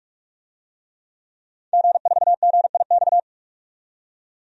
Morse code, heard as a keyed tone, about 46 wpm, 700 Hz. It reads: G4GIX
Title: DXCC-100 Call Signs - 5 Characters - 45wpm